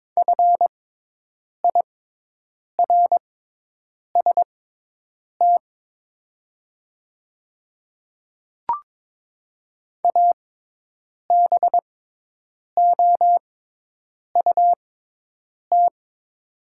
Morse code, heard as a keyed tone, about 22 wpm, spaced as for 8 wpm, 700 Hz. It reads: FIRST ABOUT
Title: 6x Character Spacing - Top 300 Words - 22wpm